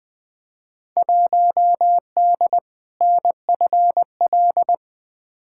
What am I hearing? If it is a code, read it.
Morse code, heard as a keyed tone, about 21 wpm, 700 Hz. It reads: 1D NFL
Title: Field Day Exchanges - 20wpm